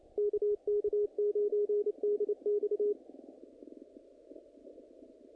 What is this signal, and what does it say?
Morse code, heard as a keyed tone, about 29 wpm, 420 Hz, with radio noise. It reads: KK9DX
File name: contests / naqp-cw-k5tr-op-k5pi- -1900